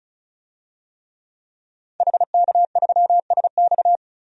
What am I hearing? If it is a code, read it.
Morse code, heard as a keyed tone, about 34 wpm, 700 Hz. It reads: HK3SX
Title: DXCC-100 Call Signs - 5 Characters - 35wpm